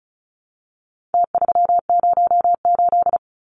Morse code, heard as a keyed tone, about 34 wpm, 700 Hz. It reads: T308
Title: Numbers - Contest Style - 4-Digit 35wpm